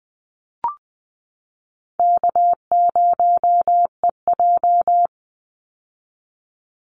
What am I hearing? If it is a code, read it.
Morse code, heard as a keyed tone, about 20 wpm, 700 Hz. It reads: K0EJ
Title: Easy Call Signs - 20wpm